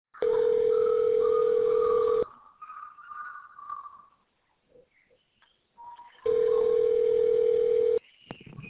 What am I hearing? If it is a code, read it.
Ringback tone (North America), 440 + 480 Hz.